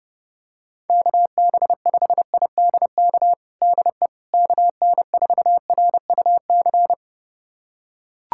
Morse code, heard as a keyed tone, about 30 wpm, 700 Hz. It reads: KB5IDK DE KN4RUC
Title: USA Callsign DE Callsign - 6 Characters - 30wpm